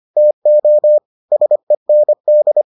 Morse code, heard as a keyed tone, about 25 wpm, 600 Hz. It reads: TO SEND